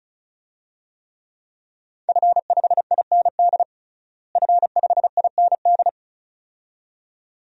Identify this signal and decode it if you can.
Morse code, heard as a keyed tone, about 36 wpm, 700 Hz. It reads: F5IND F5IND
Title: DXCC-100 Call Signs - 5 Characters - Repeated 2x - 35wpm